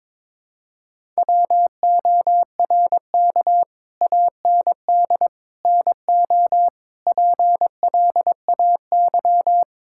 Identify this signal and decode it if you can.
Morse code, heard as a keyed tone, about 22 wpm, 700 Hz. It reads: WORK AND NO PLAY
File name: practice / 22wpm / Sentences with 8-Word Phrases - 22wpm